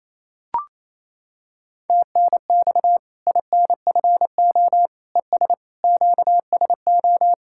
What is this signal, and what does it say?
Morse code, heard as a keyed tone, about 28 wpm, 700 Hz. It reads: TNX INFO ES QSO
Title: Practice QSO - Exchange 4 - Ending - 28wpm